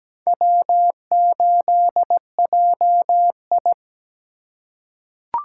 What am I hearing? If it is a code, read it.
Morse code, heard as a keyed tone, about 17 wpm, 700 Hz. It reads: W8JI
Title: USA Call Signs - 4 Characters - 3x Character Spacing - 17wpm